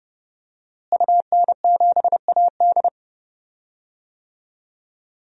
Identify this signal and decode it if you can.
Morse code, heard as a keyed tone, about 30 wpm, 700 Hz. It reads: UN7AD
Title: DXCC-100 Call Signs - 5 Characters - Repeated 3x - 30wpm